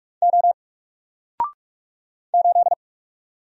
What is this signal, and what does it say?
Morse code, heard as a keyed tone, about 44 wpm, 700 Hz. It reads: O 8
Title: Single Character (Letter:Number:Punctuation:Prosign) Alt 45wpm